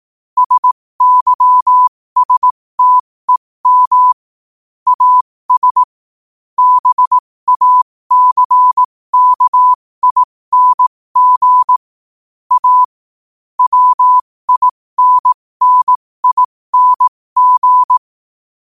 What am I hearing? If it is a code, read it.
Morse code, heard as a keyed tone, about 18 wpm, spaced as for 15 wpm, 1000 Hz. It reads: SYSTEM AS BACKING A WINNING
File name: qotdfiles / 15wpm